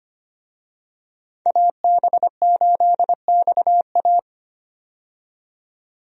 Morse code, heard as a keyed tone, about 25 wpm, 700 Hz. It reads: AB8XA
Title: USA Call Signs - 5 Characters - 25wpm